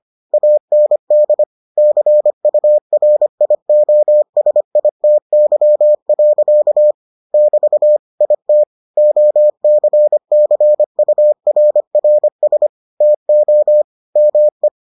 Morse code, heard as a keyed tone, about 25 wpm, 600 Hz. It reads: AND CURIOSITY. <BT> IT OCCURRS TO ME